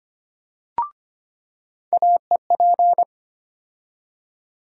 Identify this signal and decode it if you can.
Morse code, heard as a keyed tone, about 25 wpm, 700 Hz. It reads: AEP
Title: USA Call Sign Postfixes - 3 Letters 25wpm